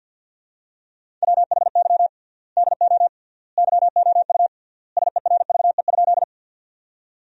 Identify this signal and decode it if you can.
Morse code, heard as a keyed tone, about 52 wpm, 700 Hz. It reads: WHY DO YOU SERVE?
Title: 3x Word-Spacing - Sentences from Top 400 Words - 50wpm